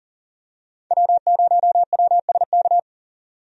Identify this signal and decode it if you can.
Morse code, heard as a keyed tone, about 40 wpm, 700 Hz. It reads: W0WSK
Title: USA Call Signs - 5 Characters - 40wpm